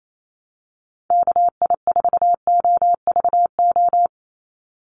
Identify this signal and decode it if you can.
Morse code, heard as a keyed tone, about 28 wpm, 700 Hz. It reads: KI4OVO
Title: USA Call Signs 28wpm